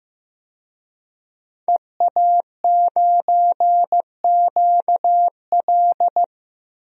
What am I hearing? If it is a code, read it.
Morse code, heard as a keyed tone, about 15 wpm, 700 Hz. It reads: EA9QL
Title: DXCC-100 Call Signs - 5 Characters - Repeated 3x - 15wpm